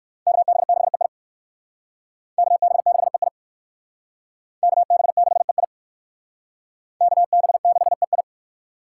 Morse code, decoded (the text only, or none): KB6EI KB6EI KB6EI KB6EI